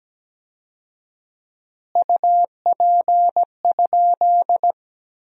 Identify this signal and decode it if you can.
Morse code, heard as a keyed tone, about 17 wpm, 700 Hz. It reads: UP?
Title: 4x Word-Spacing - Sentences from Top 500 Words - 17wpm